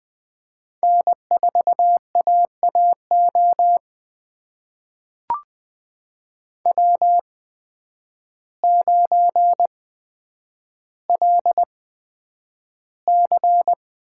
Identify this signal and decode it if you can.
Morse code, heard as a keyed tone, about 20 wpm, spaced as for 9 wpm, 700 Hz. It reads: N4AAO W9LC
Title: USA Call Signs - 5 Characters - 8x Character Spacing - 20wpm